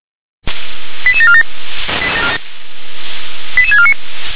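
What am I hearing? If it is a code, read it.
Five-tone selective call (ZVEI), heard as quick runs of five tones, tones of about 70 ms. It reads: 88747 88649
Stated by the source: ZVEI five-tone selective call 88852 received on SDR.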